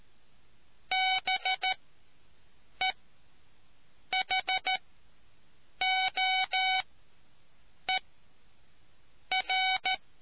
Morse code, heard as a keyed tone, about 13 wpm, spaced as for 6 wpm, 750 Hz. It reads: BEHOER